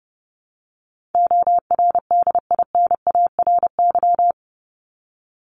Morse code, heard as a keyed tone, about 30 wpm, 700 Hz. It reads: ORDINARY
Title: Top 1751-2000 Words 30wpm